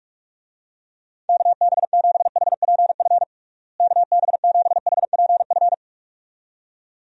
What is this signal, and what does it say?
Morse code, heard as a keyed tone, about 46 wpm, 700 Hz. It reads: KB7HPF KB7HPF
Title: USA Call Signs - 6 Characters - Repeated 2x - 45wpm